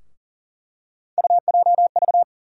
Morse code, heard as a keyed tone, about 40 wpm, 700 Hz. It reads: UJV
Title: USA Call Sign Postfixes - 3 Letters 40wpm